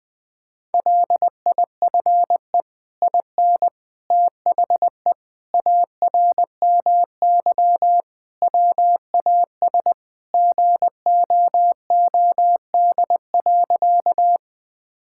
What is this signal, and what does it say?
Morse code, heard as a keyed tone, about 20 wpm, 700 Hz. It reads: LIFE IN THE ARMY WAS GOOD.
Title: Sentences from Top 1000 Words - Review Part 5 - 20wpm